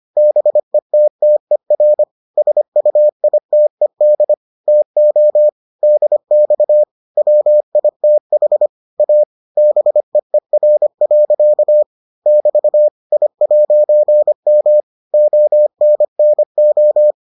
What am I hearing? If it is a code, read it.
Morse code, heard as a keyed tone, about 25 wpm, 600 Hz. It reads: BETTER SUITED TO DX WITH A BEER. <BT> I'M ONNO